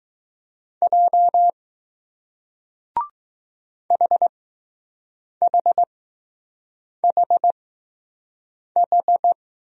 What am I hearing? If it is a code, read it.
Morse code, heard as a keyed tone, about 20 wpm, spaced as for 6 wpm, 700 Hz. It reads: J HHHH